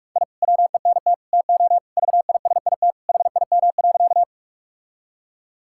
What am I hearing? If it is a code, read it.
Morse code, heard as a keyed tone, about 41 wpm, 700 Hz. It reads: I WENT TO VISIT HIM.
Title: Sentences from Top 1000 Words - Review Part 6 - 45wpm